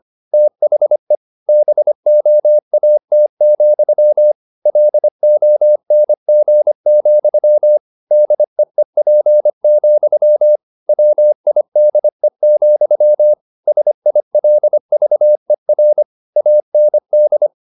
Morse code, heard as a keyed tone, about 25 wpm, 600 Hz. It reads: THE BOAT, LONG, DEEP, WIDE, SILVER AND